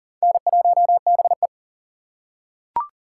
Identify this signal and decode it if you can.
Morse code, heard as a keyed tone, about 40 wpm, 700 Hz. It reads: N1BE
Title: USA Call Signs - 4 Characters - 3x Character Spacing - 40wpm